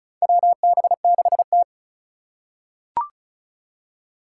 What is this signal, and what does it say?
Morse code, heard as a keyed tone, about 36 wpm, 700 Hz. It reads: WB6T
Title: USA Call Signs - 4 Characters - 2x Character Spacing - 35wpm